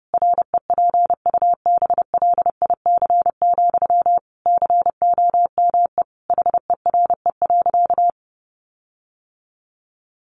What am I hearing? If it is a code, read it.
Morse code, heard as a keyed tone, about 28 wpm, 700 Hz. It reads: REPUBLIC, COME HERE.
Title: Sentences from Top 1500 Words 30wpm